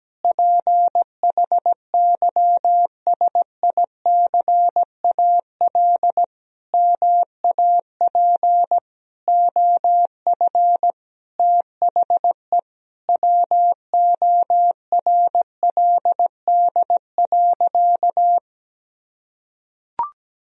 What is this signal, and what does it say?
Morse code, heard as a keyed tone, about 17 wpm, 700 Hz. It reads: PHYSICAL MAP OF THE WORLD.
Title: Sentences from Top 1000 Words - Review Part 4 - 17wpm